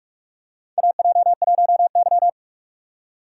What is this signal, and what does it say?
Morse code, heard as a keyed tone, about 46 wpm, 700 Hz. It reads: AJ1Y
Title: USA Call Signs - 4 Characters - 2x Character Spacing - 45wpm